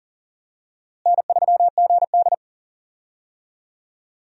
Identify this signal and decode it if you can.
Morse code, heard as a keyed tone, about 40 wpm, 700 Hz. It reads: N3GD
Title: USA Call Signs - 4 Characters - Repeated 3x - 40wpm